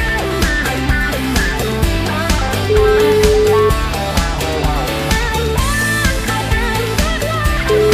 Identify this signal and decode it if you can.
Ringback tone (Europe, 425 Hz), with noise.